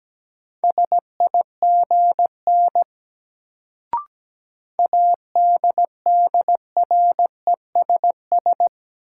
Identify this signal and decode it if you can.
Morse code, heard as a keyed tone, about 17 wpm, 700 Hz. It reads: SIGN ADDRESS
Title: Sentences from Top 1000 Words - New Word Drill - 17wpm